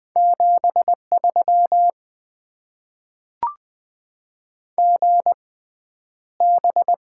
Morse code, heard as a keyed tone, about 20 wpm, spaced as for 10 wpm, 700 Hz. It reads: 73 GB
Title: Top 90 QSO Elements - 6x Character Spacing - 20wpm